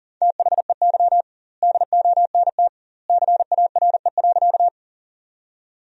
Morse code, heard as a keyed tone, about 40 wpm, 700 Hz. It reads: THEY DONT CARE.
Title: 2x Word-Spacing - Sentences from Top 300 Words - 40wpm